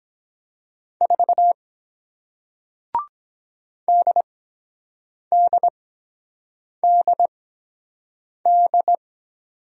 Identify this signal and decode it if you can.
Morse code, heard as a keyed tone, about 23 wpm, spaced as for 7 wpm, 700 Hz. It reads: 4 DDDD